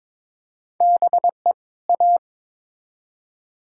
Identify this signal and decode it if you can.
Morse code, heard as a keyed tone, about 22 wpm, 700 Hz. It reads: BE A
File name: Build A Phrase - 2-Word Phrases - 22wpm